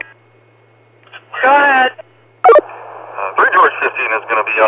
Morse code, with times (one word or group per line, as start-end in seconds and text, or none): none